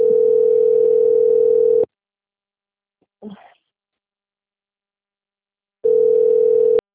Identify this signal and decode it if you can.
Ringback tone (North America), 440 + 480 Hz.